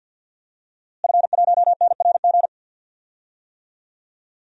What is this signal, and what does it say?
Morse code, heard as a keyed tone, about 48 wpm, 700 Hz. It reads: F1NRG
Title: DXCC-100 Call Signs - 5 Characters - Repeated 3x - 50wpm